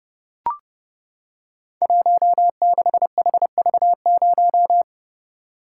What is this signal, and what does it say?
Morse code, heard as a keyed tone, about 30 wpm, 700 Hz. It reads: 16HV0
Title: Code Groups - 5 Letter:Numbers - 30wpm